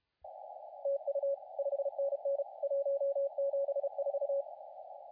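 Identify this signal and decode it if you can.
Morse code, heard as a keyed tone, about 32 wpm, 560 Hz, with radio noise. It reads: TU 5NN 174
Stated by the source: CW contest audio recorded at LA4C during the SAC CW contest.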